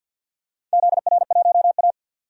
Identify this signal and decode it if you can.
Morse code, heard as a keyed tone, about 53 wpm, 700 Hz. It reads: ZR1A
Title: DXCC-100 Call Signs - 4 Characters - 50wpm